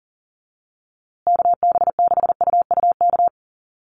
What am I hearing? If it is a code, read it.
Morse code, heard as a keyed tone, about 40 wpm, 700 Hz. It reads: KB6UUK
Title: USA Call Signs Repeated 3x 40wpm